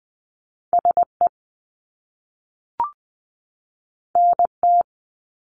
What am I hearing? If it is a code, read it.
Morse code, heard as a keyed tone, about 20 wpm, 700 Hz. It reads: SE NT
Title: Top Two Letter Combinations in Words 20wpm